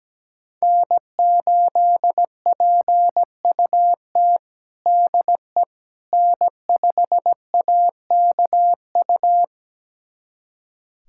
Morse code, heard as a keyed tone, about 17 wpm, 700 Hz. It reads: N8PUT DE N5AKU
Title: USA Callsign DE Callsign - 5 Characters - 17wpm